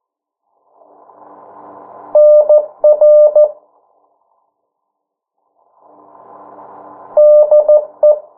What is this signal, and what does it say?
Morse code, heard as a keyed tone, about 14 wpm, 600 Hz. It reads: NR DE
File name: morse mp3 / 09 urok 8